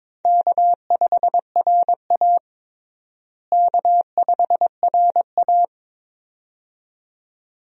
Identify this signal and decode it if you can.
Morse code, heard as a keyed tone, about 22 wpm, 700 Hz. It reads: K5RA K5RA
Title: USA Call Signs - 4 Characters - Repeated 2x - 22wpm